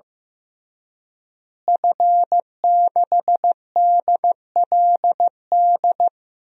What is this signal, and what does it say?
Morse code, heard as a keyed tone, about 15 wpm, 700 Hz. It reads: F6DLD
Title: DXCC-100 Call Signs - 5 Characters - Repeated 3x - 15wpm